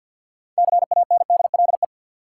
Morse code, heard as a keyed tone, about 52 wpm, 700 Hz. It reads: CANDLE
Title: Six Letter Words - 50wpm